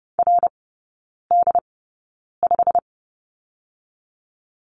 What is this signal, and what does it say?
Morse code, heard as a keyed tone, about 30 wpm, spaced as for 9 wpm, 700 Hz. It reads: RD5
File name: course / 30:10wpm / 087 Lesson 15e - Number 5 - Three Characters 30wpm